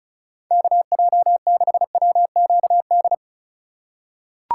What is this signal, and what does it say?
Morse code, heard as a keyed tone, about 36 wpm, 700 Hz. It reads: KJ6WQD
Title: USA Call Signs - 6 Characters - 2x Character Spacing - 35wpm